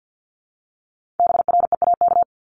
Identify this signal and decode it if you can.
Morse code, heard as a keyed tone, about 50 wpm, 700 Hz. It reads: BREAK
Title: Multi-letter Phonemes 50wpm